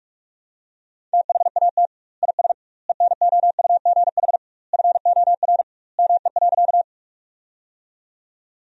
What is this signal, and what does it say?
Morse code, heard as a keyed tone, about 46 wpm, 700 Hz. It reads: THAT IS ENOUGH FOR ME.
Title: 2x Word-Spacing - Sentences from Top 400 Words - 45wpm